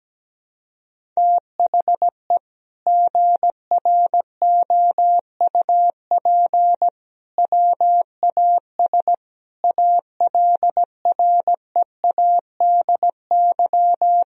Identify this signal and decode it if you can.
Morse code, heard as a keyed tone, about 17 wpm, 700 Hz. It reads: THE GROUP WAS ALREADY
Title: Sentences from Top 1000 Words - Review Part 1 - 17wpm